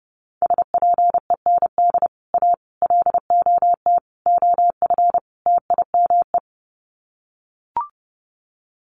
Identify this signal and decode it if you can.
Morse code, heard as a keyed tone, about 30 wpm, 700 Hz. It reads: SPEND A LOT OF TIME
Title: Sets of 5 Words 30wpm